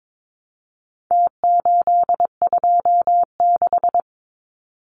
Morse code, heard as a keyed tone, about 22 wpm, 700 Hz. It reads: T826
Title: Numbers - Contest Style - 4-Digit 22wpm